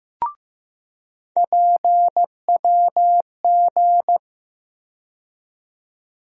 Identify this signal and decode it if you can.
Morse code, heard as a keyed tone, about 15 wpm, 700 Hz. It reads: PWG
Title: USA Call Sign Postfixes - 3 Letters 15wpm